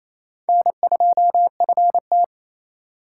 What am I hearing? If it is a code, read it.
Morse code, heard as a keyed tone, about 34 wpm, 700 Hz. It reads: N2FT
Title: Speed Racing - USA 1x2 Call Signs 28wpm